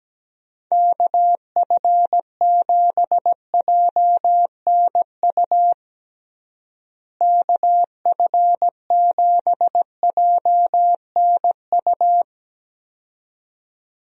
Morse code, heard as a keyed tone, about 17 wpm, 700 Hz. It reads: KF7JNU KF7JNU
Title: USA Call Signs - 6 Characters - Repeated 2x - 17wpm